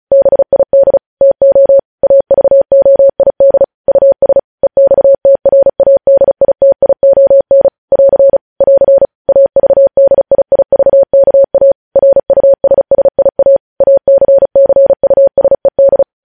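Morse code, heard as a keyed tone, about 36 wpm, 550 Hz. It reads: BID TO AVOID US EXTRADITION <AR> <AR> AVDIIVKA RUSSIA ACCUSED
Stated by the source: News headlines sent in Morse at 35 wpm.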